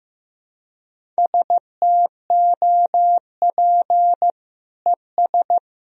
Morse code, heard as a keyed tone, about 16 wpm, 700 Hz. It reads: STOP ES
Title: Sentences from Top 300 Words and Abbreviations 15wpm